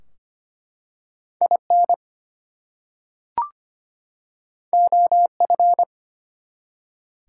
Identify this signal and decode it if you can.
Morse code, heard as a keyed tone, about 25 wpm, 700 Hz. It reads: IN OF
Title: Two Letter Words - 25wpm